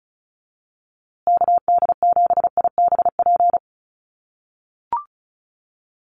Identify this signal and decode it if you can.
Morse code, heard as a keyed tone, about 34 wpm, 700 Hz. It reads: KD7IBP
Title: USA Call Signs Repeated 3x 35wpm